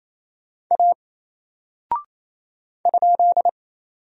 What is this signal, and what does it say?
Morse code, heard as a keyed tone, about 27 wpm, 700 Hz. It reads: A ?
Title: Single Character (Letter:Number:Punctuation:Prosign) 28wpm